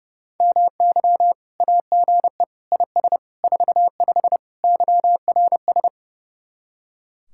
Morse code, heard as a keyed tone, about 36 wpm, 700 Hz. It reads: MY AGE IS 45 YRS
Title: Speed-Racing Common QSO Phrases Encore - 30wpm